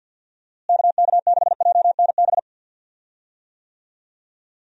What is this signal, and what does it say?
Morse code, heard as a keyed tone, about 52 wpm, 700 Hz. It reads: KK6JNB
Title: USA Call Signs - 6 Characters - Repeated 3x - 50wpm